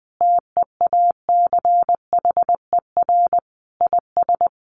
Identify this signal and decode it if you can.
Morse code, heard as a keyed tone, about 20 wpm, 700 Hz. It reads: TEACHER IS
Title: Additional Easy Phrases 20wpm